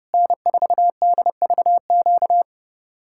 Morse code, heard as a keyed tone, about 37 wpm, 700 Hz. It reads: N4DVQ
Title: Speed-Racing Common QSO Phrases - 30wpm